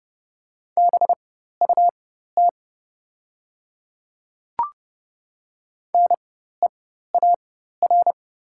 Morse code, heard as a keyed tone, about 30 wpm, spaced as for 14 wpm, 700 Hz. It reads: BUT NEAR
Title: 4x Character Spacing - Top 200 Words - 30wpm